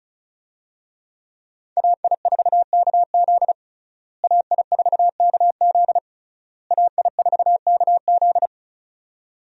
Morse code, heard as a keyed tone, about 36 wpm, 700 Hz. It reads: AI4KZ AI4KZ AI4KZ
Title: USA Call Signs - 5 Characters - Repeated 3x - 35wpm